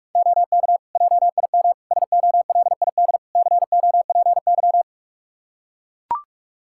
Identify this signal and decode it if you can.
Morse code, heard as a keyed tone, about 41 wpm, 700 Hz. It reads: OK JIM SOLID COPY
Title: Practice QSO - Exchange 3 - More Info - 45wpm